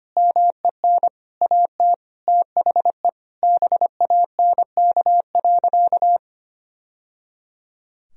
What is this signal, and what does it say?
Morse code, heard as a keyed tone, about 25 wpm, 700 Hz. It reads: MEN AT THE BANK.
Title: Sentences from Top 1000 Words - Review Part 4 - 25wpm